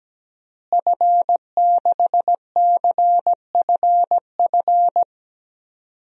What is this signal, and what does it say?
Morse code, heard as a keyed tone, about 17 wpm, 700 Hz. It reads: F6CFF
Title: DXCC-100 Call Signs - 5 Characters - Repeated 3x - 17wpm